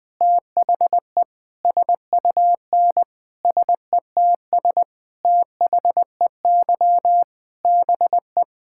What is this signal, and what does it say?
Morse code, heard as a keyed tone, about 20 wpm, 700 Hz. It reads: THE SUN SETS THEY BE
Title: Sentences with 4-Word Phrases - Spoken Beginning - 20wpm